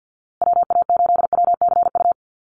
Morse code, heard as a keyed tone, about 50 wpm, 700 Hz. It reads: WA8WCU
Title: USA Call Signs Repeated 3x 50wpm